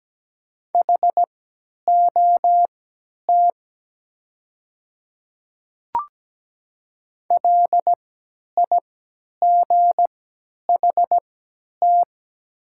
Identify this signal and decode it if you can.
Morse code, heard as a keyed tone, about 17 wpm, spaced as for 10 wpm, 700 Hz. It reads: HOT LIGHT
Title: 3x Character Spacing - Top 200 Words - 17wpm